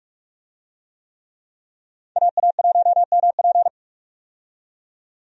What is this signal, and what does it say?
Morse code, heard as a keyed tone, about 45 wpm, 700 Hz. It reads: AA1MP
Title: USA Call Signs - 5 Characters - 45wpm